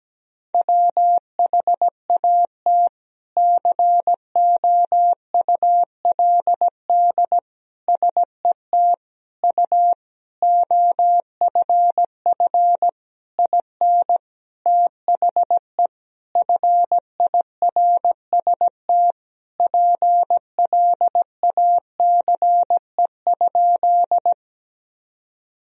Morse code, heard as a keyed tone, about 17 wpm, 700 Hz. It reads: WHAT COULD SET U OFF IN THE FIRST PLACE?
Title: Sentences from Top 200 Words and Abbreviations 17wpm